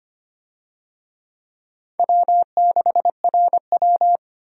Morse code, heard as a keyed tone, about 25 wpm, 700 Hz. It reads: W6RW
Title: USA Call Signs - 4 Characters - 25wpm